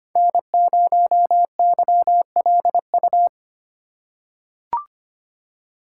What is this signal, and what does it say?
Morse code, heard as a keyed tone, about 25 wpm, 700 Hz. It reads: N0YLU
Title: USA Call Signs - 5 Characters - 3x Character Spacing - 25wpm